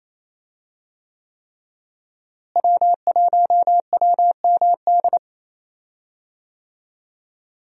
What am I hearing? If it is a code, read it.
Morse code, heard as a keyed tone, about 28 wpm, 700 Hz. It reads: W1WMD
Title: USA Call Signs - 5 Characters - Repeated 3x - 28wpm